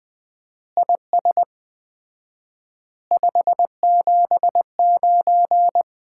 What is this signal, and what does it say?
Morse code, heard as a keyed tone, about 20 wpm, 700 Hz. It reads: IS 579
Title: Common QSO Phrases - 4x Word-Spacing - 20wpm